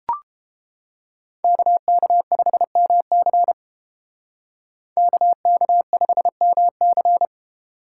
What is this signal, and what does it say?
Morse code, heard as a keyed tone, about 30 wpm, 700 Hz. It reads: KK5MC KK5MC